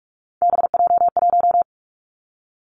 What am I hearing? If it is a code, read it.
Morse code, heard as a keyed tone, about 46 wpm, 700 Hz. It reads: BJ1
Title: Three Challenging Letter-Number Combinations 45wpm